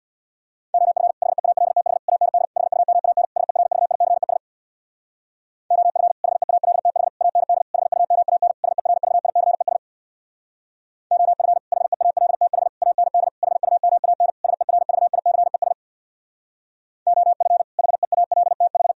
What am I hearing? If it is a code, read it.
Morse code, heard as a keyed tone, about 65 wpm, 700 Hz. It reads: OF HEALTH AND HUMAN SERVICES OF HEALTH AND HUMAN SERVICES OF HEALTH AND HUMAN SERVICES OF HEALTH